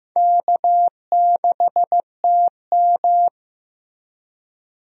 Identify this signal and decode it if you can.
Morse code, heard as a keyed tone, about 15 wpm, 700 Hz. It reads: K6TM
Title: USA Call Signs - 4 Characters - Repeated 2x - 15wpm